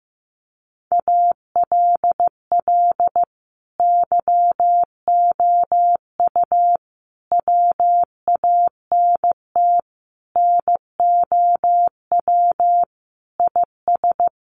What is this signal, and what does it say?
Morse code, heard as a keyed tone, about 14 wpm, 700 Hz. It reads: ALL YOU WANT NOW IS
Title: Sentences from Top 1200 Words 15wpm